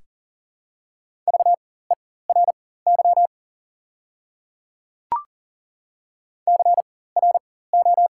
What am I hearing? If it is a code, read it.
Morse code, heard as a keyed tone, about 40 wpm, spaced as for 19 wpm, 700 Hz. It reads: VERY CRO
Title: 4x Character Spacing - Top 500 Words - 40wpm